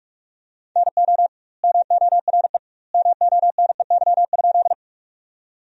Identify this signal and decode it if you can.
Morse code, heard as a keyed tone, about 46 wpm, 700 Hz. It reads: NO MORE MONEY?
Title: 2x Word-Spacing - Sentences from Top 400 Words - 45wpm